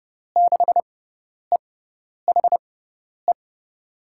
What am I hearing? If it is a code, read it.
Morse code, heard as a keyed tone, about 29 wpm, spaced as for 10 wpm, 700 Hz. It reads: 6EHE
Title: USA Call Signs - 6 Characters - 6x Character Spacing - 30wpm